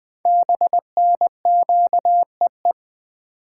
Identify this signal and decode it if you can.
Morse code, heard as a keyed tone, about 20 wpm, 700 Hz. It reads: BNQEE
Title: Code Groups - 5 Letters - 20wpm